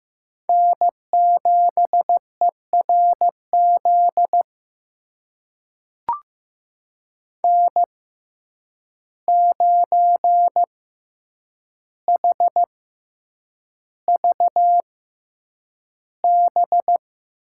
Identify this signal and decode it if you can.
Morse code, heard as a keyed tone, about 15 wpm, spaced as for 8 wpm, 700 Hz. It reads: N7ERZ N9HVB
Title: USA Call Signs - 6x Character Spacing - 15wpm